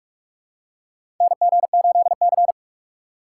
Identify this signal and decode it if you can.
Morse code, heard as a keyed tone, about 46 wpm, 700 Hz. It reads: NG8C
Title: USA Call Signs - 4 Characters - 6x Character Spacing - 45wpm